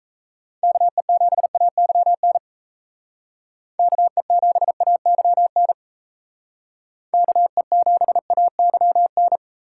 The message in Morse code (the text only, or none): KE7AYN KE7AYN KE7AYN